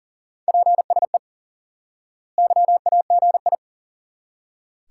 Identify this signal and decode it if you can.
Morse code, heard as a keyed tone, about 40 wpm, 700 Hz. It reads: PSE YAGI
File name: IQE Course 3 - Lesson 89 - YAGI - Two Elements - 40wpm